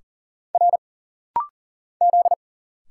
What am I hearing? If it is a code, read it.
Morse code, heard as a keyed tone, about 40 wpm, 700 Hz. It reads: R Z
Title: Single Letter-Number - Rapid-Fire 40wpm